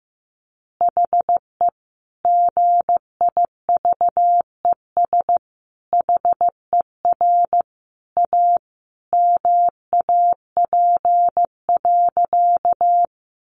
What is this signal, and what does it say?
Morse code, heard as a keyed tone, about 15 wpm, 700 Hz. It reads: HE GIVES HER A MAP.
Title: Additional Easy Phrases 15wpm